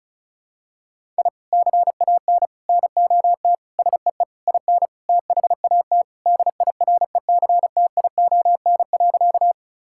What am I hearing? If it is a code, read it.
Morse code, heard as a keyed tone, about 36 wpm, 700 Hz. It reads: I CAN NOT SEE IN THAT DIRECTION.
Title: Sentences from Top 1000 Words - Review Part 5 - 35wpm